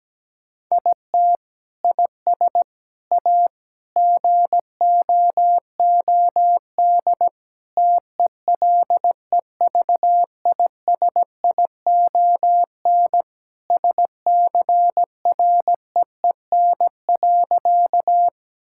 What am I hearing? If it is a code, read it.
Morse code, heard as a keyed tone, about 17 wpm, 700 Hz. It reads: IT IS A GOOD TELEVISION SCREEN.
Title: Sentences from Top 1000 Words - Review Part 6 - 17wpm